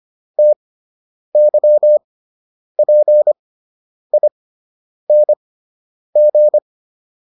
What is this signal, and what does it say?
Morse code, heard as a keyed tone, about 25 wpm, spaced as for 9 wpm, 600 Hz. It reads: TYPING